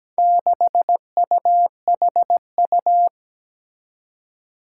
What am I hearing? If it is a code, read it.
Morse code, heard as a keyed tone, about 17 wpm, 700 Hz. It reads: 6UHU
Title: Four Challenging Letter-Number Combinations 17wpm